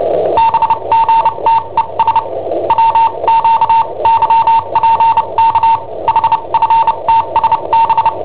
Morse code, heard as a keyed tone, about 29 wpm, spaced as for 25 wpm, 910 Hz. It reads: BGTES WQYPK HFTSB
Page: Cassetta esercizi preparata da I7OHP per IW5ELP